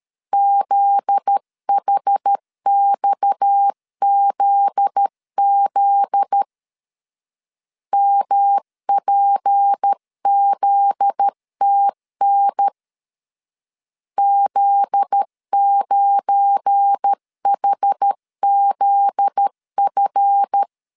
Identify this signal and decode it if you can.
Morse code, heard as a keyed tone, about 13 wpm, 800 Hz. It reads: ZHXZZ MPZTN Z9HZF
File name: LEZIONI / 7 GF9Z / 7 GF9Z 12wpm 2147